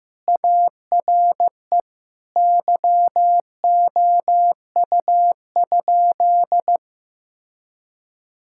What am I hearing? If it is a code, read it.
Morse code, heard as a keyed tone, about 14 wpm, 700 Hz. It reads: ARE YOU?
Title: Sentences from Top 1000 Words - Review Part 6 - 15wpm